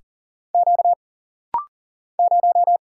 Morse code, heard as a keyed tone, about 40 wpm, 700 Hz. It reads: Q 0
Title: Single Letter-Number Alt - Rapid-Fire 40wpm